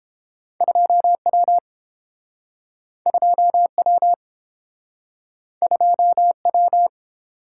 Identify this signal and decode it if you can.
Morse code, heard as a keyed tone, about 30 wpm, 700 Hz. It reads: 2W 2W 2W